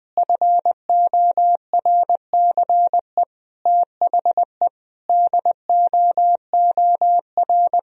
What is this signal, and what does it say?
Morse code, heard as a keyed tone, about 20 wpm, 700 Hz. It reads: FORCE THE DOOR
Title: Sentences from Top 1000 Words - Review Part 3 - 20wpm